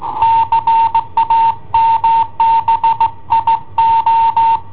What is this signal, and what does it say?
Morse code, heard as a keyed tone, about 19 wpm, 910 Hz. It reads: CAMBIO
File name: i7ohp / cassetta i7ohp B